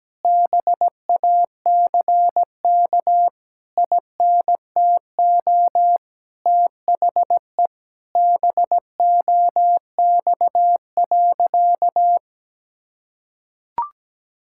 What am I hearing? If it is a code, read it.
Morse code, heard as a keyed tone, about 17 wpm, 700 Hz. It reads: BACK INTO THE BOX.
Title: Sentences from Top 1000 Words - Review Part 6 - 17wpm